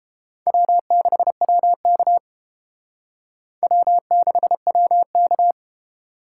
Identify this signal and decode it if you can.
Morse code, heard as a keyed tone, about 30 wpm, 700 Hz. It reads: W6WK W6WK